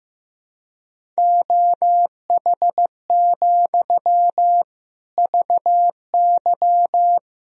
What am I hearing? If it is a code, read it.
Morse code, heard as a keyed tone, about 15 wpm, 700 Hz. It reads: OH, VY
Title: Sentences from Top 500 Words and Abbreviations 15wpm